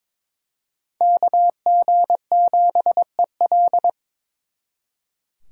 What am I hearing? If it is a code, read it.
Morse code, heard as a keyed tone, about 26 wpm, 700 Hz. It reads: KG7EL
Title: Speed Racing - USA Call Signs 22wpm